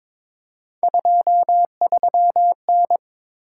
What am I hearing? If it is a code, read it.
Morse code, heard as a keyed tone, about 22 wpm, 700 Hz. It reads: 23N
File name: practice / 22wpm / Three Letter-Number Combinations - 22wpm